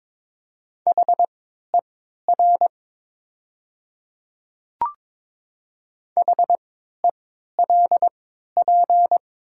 Morse code, heard as a keyed tone, about 22 wpm, spaced as for 12 wpm, 700 Hz. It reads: HER HELP
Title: 3x Character Spacing - Top 500 Words - 22wpm